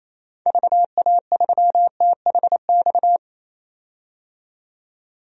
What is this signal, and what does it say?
Morse code, heard as a keyed tone, about 28 wpm, 700 Hz. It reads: VA3THX
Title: DXCC-100 Call Signs - 6 Characters - 28wpm